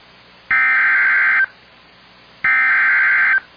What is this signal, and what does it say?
Emergency Alert System (SAME) tones (header bursts), heard as harsh buzzing bursts. It is heard from 0.5 to 3.3 s.